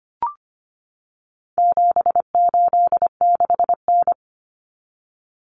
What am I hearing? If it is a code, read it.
Morse code, heard as a keyed tone, about 25 wpm, 700 Hz. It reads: 786N
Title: Common Cut Numbers - 4-Digit 25wpm